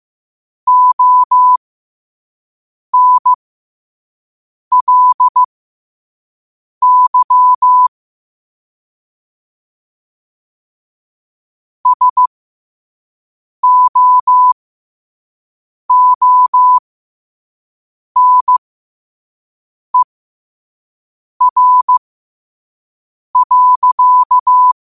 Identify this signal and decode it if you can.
Morse code, heard as a keyed tone, about 15 wpm, spaced as for 5 wpm, 1000 Hz. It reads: ONLY SOONER.